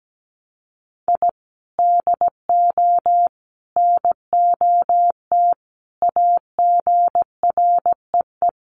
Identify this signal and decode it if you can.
Morse code, heard as a keyed tone, about 17 wpm, 700 Hz. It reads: I DO NOT AGREE
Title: Sets of 5 Words Encore 17wpm